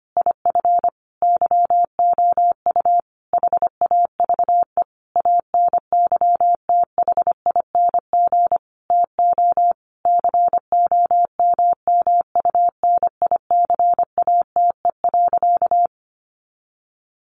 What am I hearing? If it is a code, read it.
Morse code, heard as a keyed tone, about 25 wpm, 700 Hz. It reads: IF YOU HAVE ANYTHING TO COMMUNICATE.
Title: Sentences from Top 1300 Words 25wpm